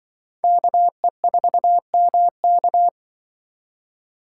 Morse code, heard as a keyed tone, about 24 wpm, 700 Hz. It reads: KE4MK
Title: Speed Racing - USA 2x2 Call Signs 20wpm